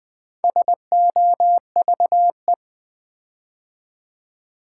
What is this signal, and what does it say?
Morse code, heard as a keyed tone, about 20 wpm, 700 Hz. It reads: SOVE
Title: Build A Word - 4 Letters At A Time - 11-Letter Words - 20wpm